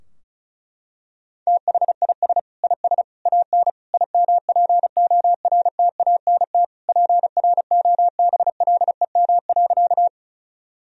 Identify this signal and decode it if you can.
Morse code, heard as a keyed tone, about 36 wpm, 700 Hz. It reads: THIS IS AN IMPORTANT PROBLEM.
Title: Sentences from Top 1000 Words - Review Part 2 - 35wpm